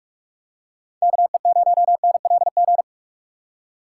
Morse code, heard as a keyed tone, about 42 wpm, 700 Hz. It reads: KE0NLG
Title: USA Call Signs - 6 Characters - 3x Character Spacing - 45wpm